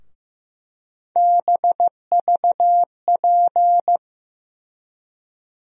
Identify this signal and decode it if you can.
Morse code, heard as a keyed tone, about 15 wpm, 700 Hz. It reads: BVP
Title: USA Call Sign Postfixes - 3 Letters 15wpm